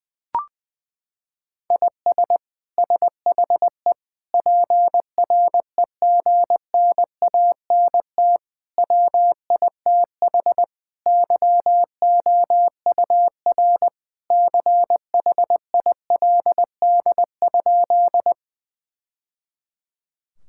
Morse code, heard as a keyed tone, about 20 wpm, 700 Hz. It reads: IS SHE PREGNANT WITH YOUR CHILD?
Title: Sentences from Top 1000 Words - Review Part 6 - 20wpm